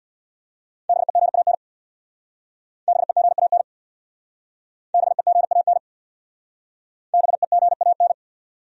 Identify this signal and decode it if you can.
Morse code, heard as a keyed tone, about 65 wpm, 700 Hz. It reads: BEGAN BEGAN BEGAN BEGAN